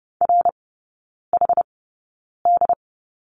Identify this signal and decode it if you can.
Morse code, heard as a keyed tone, about 30 wpm, spaced as for 9 wpm, 700 Hz. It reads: RHD
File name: 081 Lesson 14e - Number 2 - Three Characters 30wpm